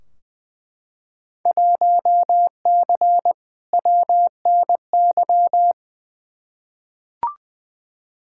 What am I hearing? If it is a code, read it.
Morse code, heard as a keyed tone, about 20 wpm, 700 Hz. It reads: 1C WNY
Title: Field Day Exchanges Encore - 20wpm